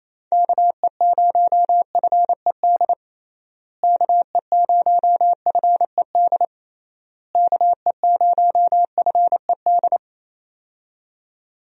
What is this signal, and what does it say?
Morse code, heard as a keyed tone, about 27 wpm, 700 Hz. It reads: KE0FED KE0FED KE0FED
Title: USA Call Signs - 6 Characters - Repeated 3x - 28wpm